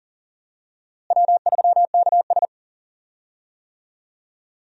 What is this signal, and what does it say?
Morse code, heard as a keyed tone, about 38 wpm, 700 Hz. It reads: W3KS
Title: USA Call Signs - 4 Characters - Repeated 3x - 40wpm